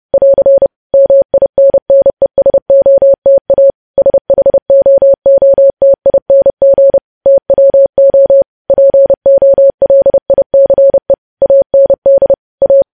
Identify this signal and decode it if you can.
Morse code, heard as a keyed tone, about 30 wpm, 550 Hz. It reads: <AR> MINNESOTA SHOOTING TWO POLICE AND A